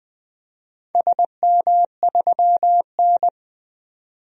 Morse code, heard as a keyed tone, about 20 wpm, 700 Hz. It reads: SM3N
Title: DXCC-100 Call Signs - 4 Characters - 20wpm